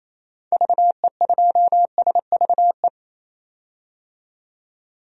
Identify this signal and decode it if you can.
Morse code, heard as a keyed tone, about 27 wpm, 700 Hz. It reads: VE2SVE
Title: DXCC-100 Call Signs - 6 Characters - 28wpm